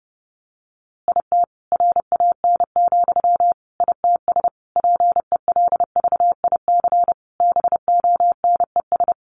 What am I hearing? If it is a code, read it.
Morse code, heard as a keyed tone, about 30 wpm, 700 Hz. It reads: IT RAN, ITS PELVIC BONES
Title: A Sound Of Thunder 30wpm